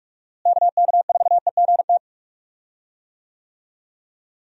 Morse code, heard as a keyed tone, about 46 wpm, 700 Hz. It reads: KK4EGT
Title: USA Call Signs - 6 Characters - Repeated 3x - 45wpm